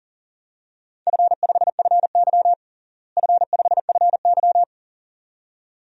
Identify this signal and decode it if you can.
Morse code, heard as a keyed tone, about 40 wpm, 700 Hz. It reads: F5FY F5FY
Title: DXCC-100 Call Signs - 4 Characters - Repeated 2x - 40wpm